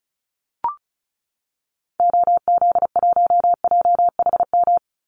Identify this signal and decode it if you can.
Morse code, heard as a keyed tone, about 34 wpm, 700 Hz. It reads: OZ1JHM
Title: Call Signs Repeated 2x - 35wpm